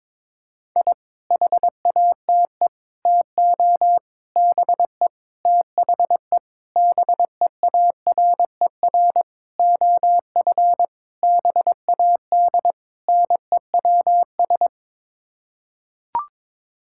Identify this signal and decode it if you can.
Morse code, heard as a keyed tone, about 22 wpm, 700 Hz. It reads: I HATE TO BE THE BEARER OF BAD NEWS
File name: Sets of 9 Words - 22wpm